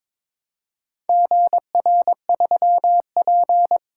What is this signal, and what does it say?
Morse code, heard as a keyed tone, about 22 wpm, 700 Hz. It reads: GR3P
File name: practice / 22wpm / Four Letter-Number Combinations - 22wpm